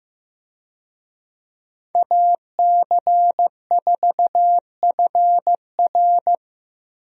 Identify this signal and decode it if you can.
Morse code, heard as a keyed tone, about 15 wpm, 700 Hz. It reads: AC4FR
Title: USA Call Signs - 5 Characters - Repeated 2x - 15wpm